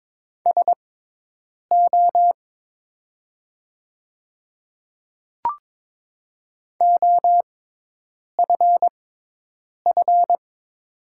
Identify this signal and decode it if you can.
Morse code, heard as a keyed tone, about 22 wpm, spaced as for 8 wpm, 700 Hz. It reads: SO OFF
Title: 6x Character Spacing - Top 200 Words - 22wpm